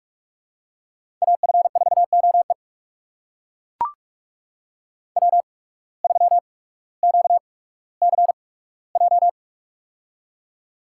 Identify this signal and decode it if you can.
Morse code, heard as a keyed tone, about 43 wpm, spaced as for 19 wpm, 700 Hz. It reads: AF4OE W3QCJ
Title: USA Call Signs - 5 Characters - 8x Character Spacing - 45wpm